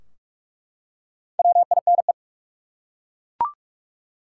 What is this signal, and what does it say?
Morse code, heard as a keyed tone, about 46 wpm, 700 Hz. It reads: WINE
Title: Four Letter Words - 45wpm